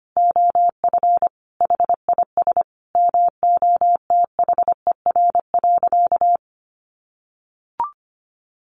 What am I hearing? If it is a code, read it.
Morse code, heard as a keyed tone, about 25 wpm, 700 Hz. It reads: OF HIS MOTHER.
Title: Sentences from Top 700 Words 25wpm